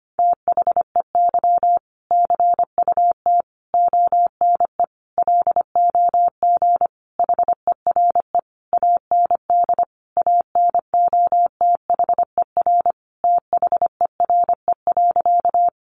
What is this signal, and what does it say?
Morse code, heard as a keyed tone, about 25 wpm, 700 Hz. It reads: THEY CUT ONE LOG HERE AND ANOTHER THERE.
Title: Sentences from Top 1400 Words 25wpm